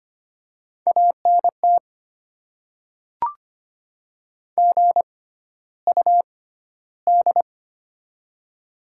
Morse code, heard as a keyed tone, about 25 wpm, spaced as for 13 wpm, 690 Hz. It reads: ANT GUD
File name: Top 20 QSO Elements - 6x Character Spacing - 25wpm